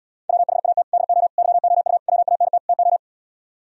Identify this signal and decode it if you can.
Morse code, heard as a keyed tone, about 77 wpm, 700 Hz. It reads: WHAT DO YOU WANT TO